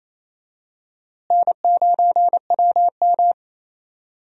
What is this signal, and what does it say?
Morse code, heard as a keyed tone, about 27 wpm, 700 Hz. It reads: N9WM
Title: USA Call Signs - 4 Characters - Repeated 2x - 28wpm